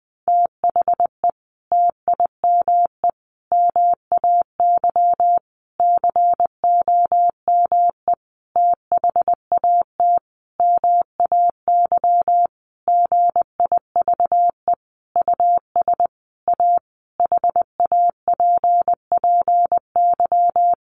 Morse code, heard as a keyed tone, about 20 wpm, 700 Hz. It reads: THE TIME MAY COME THAT MAY GIVE US A HAPPY
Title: Sentences from Top 1100 Words 20wpm